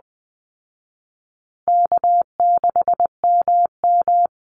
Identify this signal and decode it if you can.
Morse code, heard as a keyed tone, about 20 wpm, 700 Hz. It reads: K6MM
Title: Easy Call Signs Repeated 3x - 20wpm